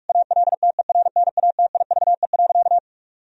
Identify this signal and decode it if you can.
Morse code, heard as a keyed tone, about 46 wpm, 700 Hz. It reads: ALTERNATIVE.
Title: Sentences from Top 1000 Words - Review Part 4 - 45wpm